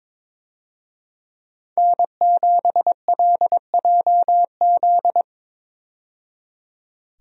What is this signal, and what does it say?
Morse code, heard as a keyed tone, about 22 wpm, 700 Hz. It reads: N7LJZ
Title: USA Call Signs - 5 Characters - 22wpm